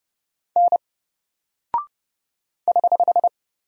Morse code, heard as a keyed tone, about 30 wpm, 700 Hz. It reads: N <HH>
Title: Single Character (Letter:Number:Punctuation:Prosign) 30wpm